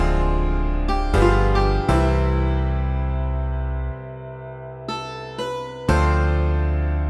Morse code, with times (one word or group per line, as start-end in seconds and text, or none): none